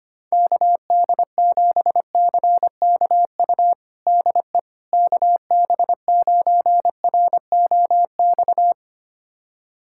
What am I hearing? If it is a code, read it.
Morse code, heard as a keyed tone, about 25 wpm, 700 Hz. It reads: KD7CKU DE KB9ROX
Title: USA Callsign DE Callsign 25wpm